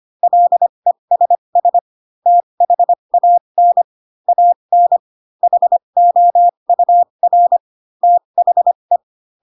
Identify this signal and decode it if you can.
Morse code, heard as a keyed tone, about 25 wpm, spaced as for 22 wpm, 700 Hz. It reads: LESS THAN AN HOUR THE